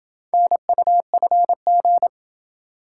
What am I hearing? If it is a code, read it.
Morse code, heard as a keyed tone, about 26 wpm, 700 Hz. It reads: NUFG